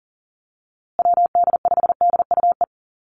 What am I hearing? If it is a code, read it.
Morse code, heard as a keyed tone, about 40 wpm, 700 Hz. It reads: WD5DUE
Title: USA Call Signs Repeated 3x 40wpm